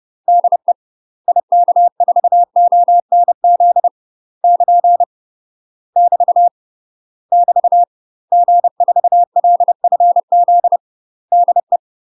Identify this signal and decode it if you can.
Morse code, heard as a keyed tone, about 30 wpm, 700 Hz. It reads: DE IK4ONZ <KN> <BT> <BT> G4LFZ DE
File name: UK MASTER QSOs TRAINING / 30WPM / UK MASTER QSOs 0046